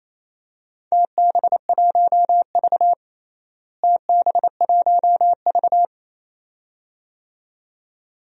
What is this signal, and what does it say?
Morse code, heard as a keyed tone, about 28 wpm, 700 Hz. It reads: TB1V TB1V
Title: DXCC-100 Call Signs - 4 Characters - Repeated 2x - 28wpm